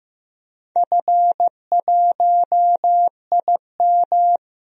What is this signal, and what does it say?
Morse code, heard as a keyed tone, about 16 wpm, 700 Hz. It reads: F1IM
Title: DXCC-100 Call Signs - 4 Characters - Repeated 3x - 15wpm